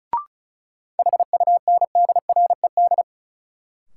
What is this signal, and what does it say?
Morse code, heard as a keyed tone, about 36 wpm, 700 Hz. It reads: HUNDRED
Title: Sentences from Top 400 Words - New Word Drill - 35wpm